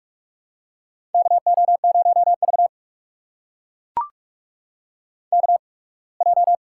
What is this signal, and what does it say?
Morse code, heard as a keyed tone, about 44 wpm, spaced as for 27 wpm, 700 Hz. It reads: KO0V KJ
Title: USA Call Signs - 8x Character Spacing - 45wpm